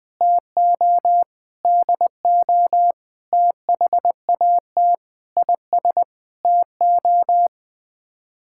Morse code, heard as a keyed tone, about 20 wpm, 700 Hz. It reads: TO DO THAT IS TO
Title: Sets of 9 Words - 20wpm